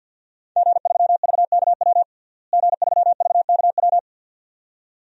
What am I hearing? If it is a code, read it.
Morse code, heard as a keyed tone, about 46 wpm, 700 Hz. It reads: G3VKW G3VKW
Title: DXCC-100 Call Signs - 5 Characters - Repeated 2x - 50wpm